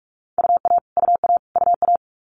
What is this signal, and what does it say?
Morse code, heard as a keyed tone, about 46 wpm, 700 Hz. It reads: UA UA UA
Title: Multi-letter Phonemes Repeated 3x 45wpm